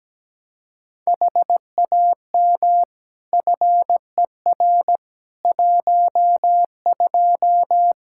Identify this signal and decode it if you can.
Morse code, heard as a keyed tone, about 17 wpm, 700 Hz. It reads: HAM FER 12
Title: Practice QSO - Exchange 3 - More Info - 17wpm